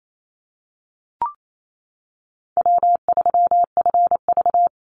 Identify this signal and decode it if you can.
Morse code, heard as a keyed tone, about 28 wpm, 700 Hz. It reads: W3FV
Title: Easy Call Signs - 28wpm